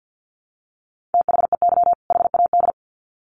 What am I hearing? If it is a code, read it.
Morse code, heard as a keyed tone, about 50 wpm, 700 Hz. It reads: THEY HAD
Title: Sets of 2 Words 50wpm